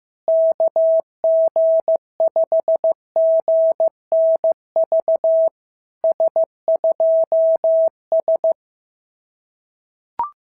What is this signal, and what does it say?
Morse code, heard as a keyed tone, about 14 wpm, 650 Hz. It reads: KG5GNV S2S
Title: SOTA QSO Practice 15wpm